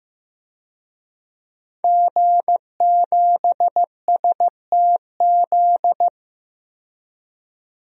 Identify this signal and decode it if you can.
Morse code, heard as a keyed tone, about 15 wpm, 700 Hz. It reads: G7STZ
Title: DXCC-100 Call Signs - 5 Characters - 15wpm